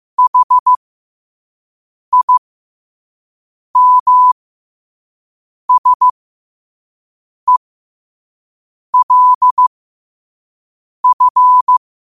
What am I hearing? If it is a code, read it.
Morse code, heard as a keyed tone, about 15 wpm, spaced as for 5 wpm, 1000 Hz. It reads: HIMSELF